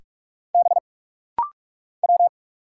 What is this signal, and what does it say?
Morse code, heard as a keyed tone, about 44 wpm, 700 Hz. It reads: B W
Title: Single Letter-Number - Rapid-Fire 45wpm